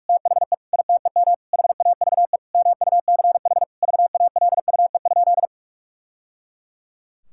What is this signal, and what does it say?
Morse code, heard as a keyed tone, about 42 wpm, 700 Hz. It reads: THE ITEM HAVE MUCH VALUE?
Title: Sentences from Top 1000 Words - Review Part 6 - 45wpm